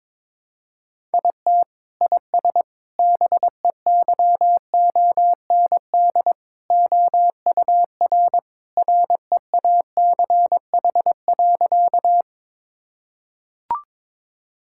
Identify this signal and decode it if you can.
Morse code, heard as a keyed tone, about 22 wpm, 700 Hz. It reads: IT IS BEYOND OUR REACH.
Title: Sentences from Top 1000 Words - Review Part 6 - 22wpm